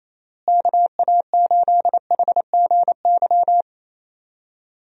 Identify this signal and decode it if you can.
Morse code, heard as a keyed tone, about 27 wpm, 700 Hz. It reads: KA8HGY
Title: USA Call Signs - 6 Characters - Repeated 2x - 28wpm